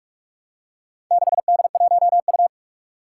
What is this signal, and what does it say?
Morse code, heard as a keyed tone, about 44 wpm, 700 Hz. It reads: 6D1U
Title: Four Challenging Letter-Number Combinations 45wpm